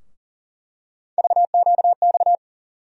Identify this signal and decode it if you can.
Morse code, heard as a keyed tone, about 40 wpm, 700 Hz. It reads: VQX
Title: USA Call Sign Postfixes - 3 Letters 40wpm